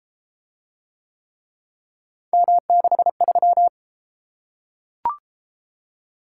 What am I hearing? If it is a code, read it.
Morse code, heard as a keyed tone, about 32 wpm, 700 Hz. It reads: M63